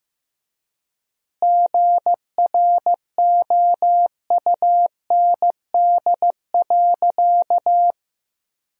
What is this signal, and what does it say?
Morse code, heard as a keyed tone, about 15 wpm, 700 Hz. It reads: GROUND.
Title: 4x Word-Spacing - Sentences from Top 500 Words - 15wpm